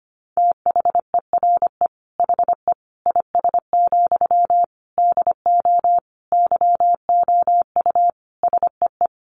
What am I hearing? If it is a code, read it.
Morse code, heard as a keyed tone, about 25 wpm, 700 Hz. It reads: THERE HE IS, DO YOU SEE
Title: Sentences from Top 100 Words 25wpm